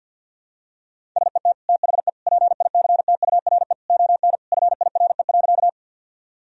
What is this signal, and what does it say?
Morse code, heard as a keyed tone, about 52 wpm, 700 Hz. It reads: SET THE PICTURE ON FIRE.
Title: Sentences from Top 1000 Words - Review Part 1 - 50wpm